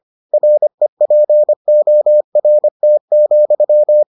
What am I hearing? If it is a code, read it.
Morse code, heard as a keyed tone, about 25 wpm, 600 Hz. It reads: REPORT,